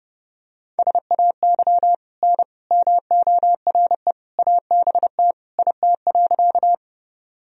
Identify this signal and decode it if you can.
Morse code, heard as a keyed tone, about 32 wpm, 700 Hz. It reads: SAY N MORE ABT IT.
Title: Sentences from Top 200 Words and Abbreviations 30wpm